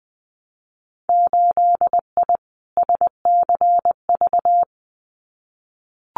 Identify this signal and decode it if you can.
Morse code, heard as a keyed tone, about 20 wpm, 700 Hz. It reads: 8I SCV
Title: Winter Field Day Exchanges Encore 20wpm